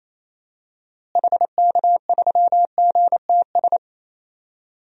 Morse code, heard as a keyed tone, about 28 wpm, 700 Hz. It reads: HK3GTS
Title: DXCC-100 Call Signs - 6 Characters - Repeated 3x - 28wpm